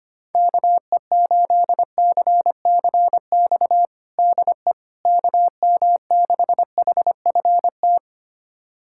Morse code, heard as a keyed tone, about 25 wpm, 700 Hz. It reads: KE8CCX DE KM6HFT
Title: USA Callsign DE Callsign - 6 Characters - 25wpm